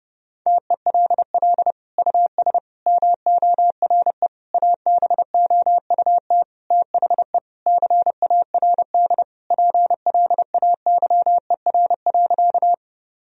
Morse code, heard as a keyed tone, about 32 wpm, 700 Hz. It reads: TELL US MORE ABOUT THE CARD PLAYER.
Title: Sentences from Top 1000 Words - Review Part 4 - 30wpm